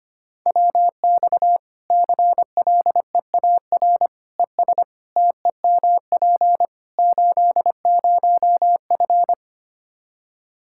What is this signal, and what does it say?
Morse code, heard as a keyed tone, about 25 wpm, 700 Hz. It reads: WX CLEAR ES TEMP 80F
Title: K1USN SST QSO Practice 25wpm